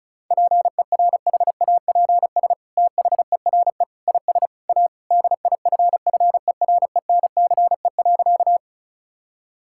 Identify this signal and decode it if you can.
Morse code, heard as a keyed tone, about 36 wpm, 700 Hz. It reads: PERHAPS THERE IS A DIFFERENCE.
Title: Sentences from Top 1000 Words - Review Part 5 - 35wpm